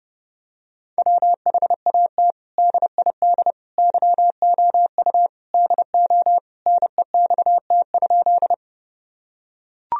Morse code, heard as a keyed tone, about 32 wpm, 700 Hz. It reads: WHAT DID YOU DO NEXT?
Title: Sentences from Top 1000 Words - Review Part 1 - 30wpm